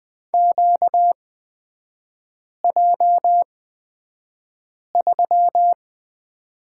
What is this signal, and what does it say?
Morse code, heard as a keyed tone, about 20 wpm, spaced as for 5 wpm, 700 Hz. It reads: QJ3
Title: 236 Lesson 38e - Letter q - Three Characters 20wpm